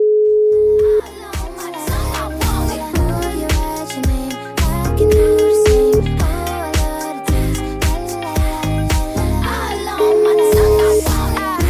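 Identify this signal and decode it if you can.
Ringback tone (Europe, 425 Hz), with noise.
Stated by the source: O2 Czech Republic ringback melody that callers hear before the call is answered.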